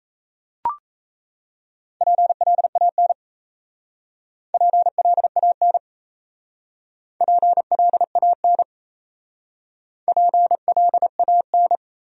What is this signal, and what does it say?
Morse code, heard as a keyed tone, about 34 wpm, 690 Hz. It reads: PLAN PLAN PLAN PLAN
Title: Speed-Racing Top 500 Words 28wpm